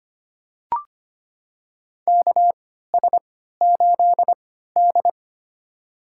Morse code, heard as a keyed tone, about 25 wpm, spaced as for 14 wpm, 700 Hz. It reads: KS8D
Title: USA Call Signs - 4 Characters - 3x Character Spacing - 25wpm